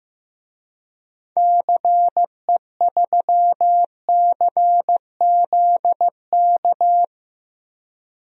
Morse code, heard as a keyed tone, about 15 wpm, 700 Hz. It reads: CE3CZK
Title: DXCC-100 Call Signs - 6 Characters - 15wpm